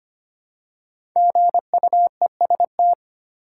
Morse code, heard as a keyed tone, about 25 wpm, 700 Hz. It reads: GUEST
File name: 5 Letter Words - First 3 Letters Given - 25wpm